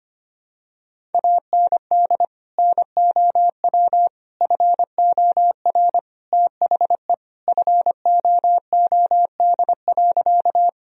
Morse code, heard as a keyed tone, about 25 wpm, 700 Hz. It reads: AND NOW FOR THE FOOD.
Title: Sentences from Top 1000 Words - Review Part 1 - 25wpm